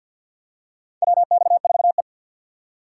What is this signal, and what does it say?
Morse code, heard as a keyed tone, about 53 wpm, 700 Hz. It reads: WX4E
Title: USA Call Signs - 4x Character Spacing - 50wpm